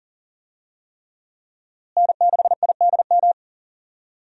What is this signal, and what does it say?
Morse code, heard as a keyed tone, about 37 wpm, 700 Hz. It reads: N6IDM
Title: USA Call Signs - 5 Characters - Repeated 3x - 40wpm